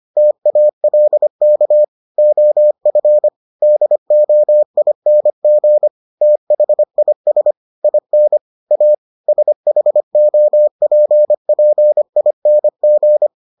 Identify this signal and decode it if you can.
Morse code, heard as a keyed tone, about 25 wpm, 600 Hz. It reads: TALK OF DOING THIS IN A SHOPPING